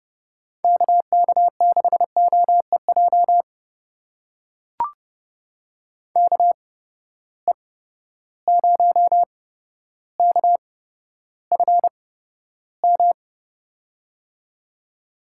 Morse code, heard as a keyed tone, about 29 wpm, spaced as for 13 wpm, 700 Hz. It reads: KK6OEJ KE0KFM
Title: USA Call Signs - 6 Characters - 8x Character Spacing - 30wpm